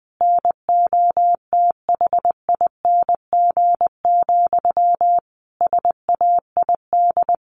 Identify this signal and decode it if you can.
Morse code, heard as a keyed tone, about 20 wpm, 700 Hz. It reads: NOTHING, SAID
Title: A Sound Of Thunder 20wpm